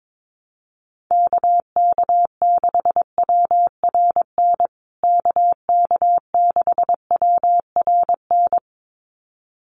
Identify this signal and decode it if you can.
Morse code, heard as a keyed tone, about 22 wpm, 700 Hz. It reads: KK6WRN KK6WRN
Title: USA Call Signs Repeated 2x 22wpm